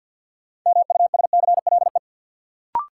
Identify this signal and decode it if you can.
Morse code, heard as a keyed tone, about 50 wpm, 700 Hz. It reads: MUSCLE
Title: Sentences from Top 1300 Words - New Word Drill - 50wpm